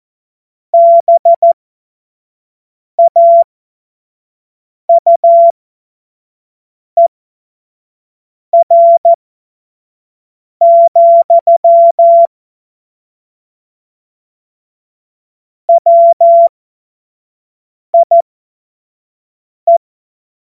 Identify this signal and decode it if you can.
Morse code, heard as a keyed tone, about 14 wpm, spaced as for 5 wpm, 680 Hz. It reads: BAUER, WIE
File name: Morsekurs / Witze / Witz 07. September 2021, 25 F70